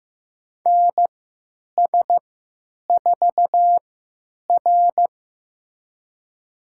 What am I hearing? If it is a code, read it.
Morse code, heard as a keyed tone, about 15 wpm, spaced as for 9 wpm, 700 Hz. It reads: NS4R
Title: USA Call Signs - 4 Characters - 3x Character Spacing - 15wpm